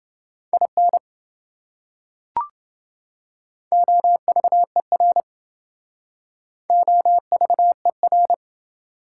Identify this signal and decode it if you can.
Morse code, heard as a keyed tone, about 28 wpm, 700 Hz. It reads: IN OVER OVER